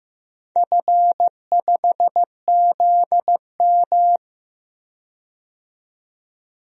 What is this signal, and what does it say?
Morse code, heard as a keyed tone, about 15 wpm, 700 Hz. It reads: F5ZM
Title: DXCC-100 Call Signs - 4 Characters - 15wpm